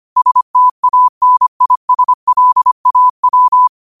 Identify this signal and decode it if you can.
Morse code, heard as a keyed tone, about 25 wpm, 1000 Hz. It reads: STANISLAW
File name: qotdfiles / 25wpm